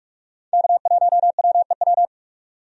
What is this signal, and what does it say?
Morse code, heard as a keyed tone, about 47 wpm, 700 Hz. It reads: K1WEW
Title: USA Call Signs - 5 Characters - 45wpm